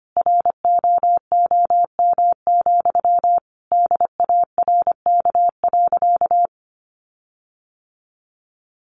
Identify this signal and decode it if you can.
Morse code, heard as a keyed tone, about 25 wpm, 700 Hz. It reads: ROOM, DARK.
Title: Sentences from Top 1400 Words 25wpm